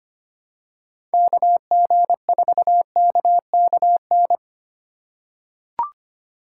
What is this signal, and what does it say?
Morse code, heard as a keyed tone, about 25 wpm, 700 Hz. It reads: KG4KKN
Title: USA Call Signs - 6 Characters - 4x Character Spacing - 25wpm